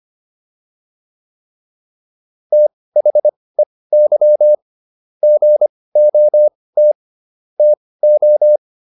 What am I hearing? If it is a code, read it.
Morse code, heard as a keyed tone, about 25 wpm, spaced as for 18 wpm, 600 Hz. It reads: THEY GOT TO